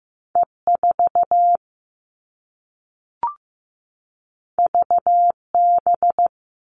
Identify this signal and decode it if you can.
Morse code, heard as a keyed tone, about 15 wpm, 700 Hz. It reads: E4 VB
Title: All Cut Numbers - 2-Digit 15wpm